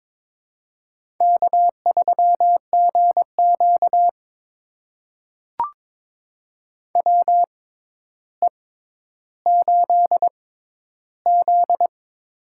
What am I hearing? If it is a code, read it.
Morse code, heard as a keyed tone, about 22 wpm, spaced as for 11 wpm, 700 Hz. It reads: K3GQ WE8Z
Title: USA Call Signs - 6x Character Spacing - 22wpm